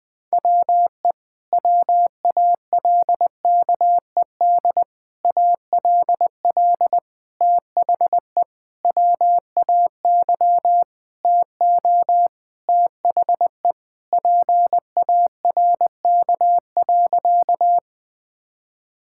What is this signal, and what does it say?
Morse code, heard as a keyed tone, about 21 wpm, 700 Hz. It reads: WE WALKED ALL THE WAY TO THE PARK.
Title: Sentences with 4-Word Phrases - 20wpm